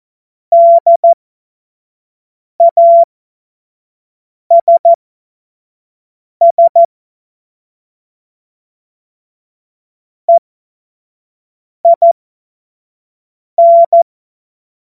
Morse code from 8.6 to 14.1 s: EIN